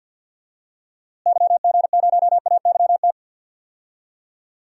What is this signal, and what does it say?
Morse code, heard as a keyed tone, about 52 wpm, 700 Hz. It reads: YG0AYT
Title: DXCC-100 Call Signs - 6 Characters - Repeated 3x - 50wpm